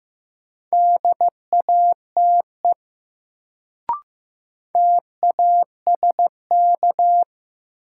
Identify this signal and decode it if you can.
Morse code, heard as a keyed tone, about 15 wpm, 700 Hz. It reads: DATE TASK
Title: Sentences from Top 1000 Words - Review Part 7 - New Word Drill - 15wpm